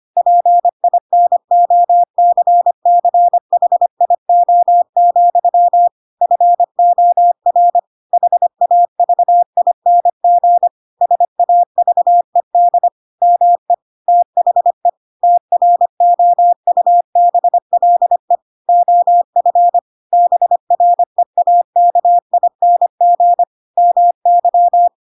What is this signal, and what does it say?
Morse code, heard as a keyed tone, about 25 wpm, 700 Hz. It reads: PINOCCHIO, FOR HAVING SAVED ME THE TROUBLE OF BREAKING MY
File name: Pinocchio / Pinocchio ch5 25 25 ik1hge